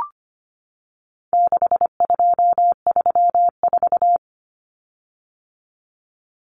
Morse code, heard as a keyed tone, about 25 wpm, 700 Hz. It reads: 6234
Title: Common Cut Numbers - 4-Digit 25wpm